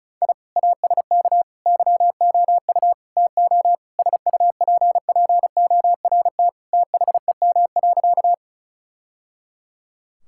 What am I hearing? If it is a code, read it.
Morse code, heard as a keyed tone, about 36 wpm, 700 Hz. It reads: I ASK YOU TO SUPPORT THEM.
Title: Sentences from Top 1000 Words - Review Part 5 - 35wpm